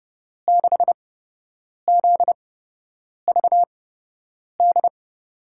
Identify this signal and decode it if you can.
Morse code, heard as a keyed tone, about 32 wpm, spaced as for 8 wpm, 700 Hz. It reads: 6ZVD
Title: USA Call Signs - 6 Characters - 8x Character Spacing - 30wpm